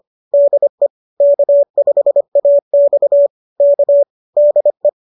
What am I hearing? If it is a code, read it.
Morse code, heard as a keyed tone, about 25 wpm, 590 Hz. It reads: DE K5AX K DE
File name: qso201 25